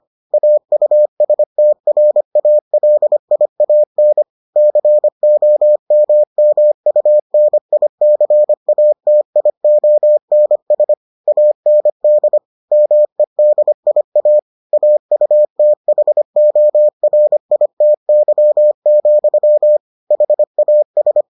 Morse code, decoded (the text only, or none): AUSTRALIAN COMMUNICATIONS AND MEDIA AUTHORITY, HAS